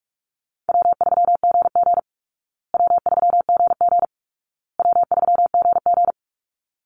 Morse code, heard as a keyed tone, about 44 wpm, 700 Hz. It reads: W3GG W3GG W3GG
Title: Easy Call Signs Repeated 3x - 45wpm